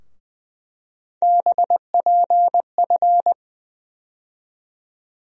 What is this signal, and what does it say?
Morse code, heard as a keyed tone, about 20 wpm, 700 Hz. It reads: BPF
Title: USA Call Sign Postfixes - 3 Letters 20wpm